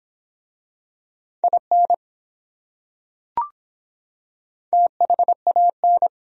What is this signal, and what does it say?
Morse code, heard as a keyed tone, about 23 wpm, 700 Hz. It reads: IN THAN